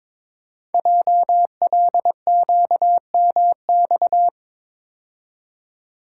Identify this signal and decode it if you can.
Morse code, heard as a keyed tone, about 23 wpm, 700 Hz. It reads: JLQMX
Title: Code Groups - 5 Letters - 22wpm